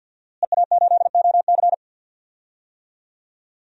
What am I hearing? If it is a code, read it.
Morse code, heard as a keyed tone, about 46 wpm, 700 Hz. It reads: EA8OC
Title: DXCC-100 Call Signs - 5 Characters - Repeated 3x - 50wpm